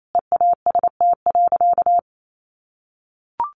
Morse code, heard as a keyed tone, about 28 wpm, 700 Hz. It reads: EAST.
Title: Sentences from Top 1300 Words 28wpm